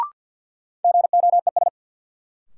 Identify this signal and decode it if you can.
Morse code, heard as a keyed tone, about 46 wpm, 700 Hz. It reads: GOES
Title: Sentences from Top 1500 Words - New Word Drill Alt - 50wpm